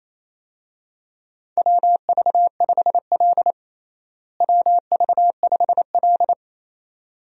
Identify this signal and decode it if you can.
Morse code, heard as a keyed tone, about 28 wpm, 700 Hz. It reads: WV5L WV5L
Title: USA Call Signs - 4 Characters - Repeated 2x - 28wpm